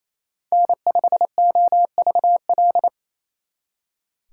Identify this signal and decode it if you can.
Morse code, heard as a keyed tone, about 33 wpm, 700 Hz. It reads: N5OVL
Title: Speed Racing - USA Call Signs 28wpm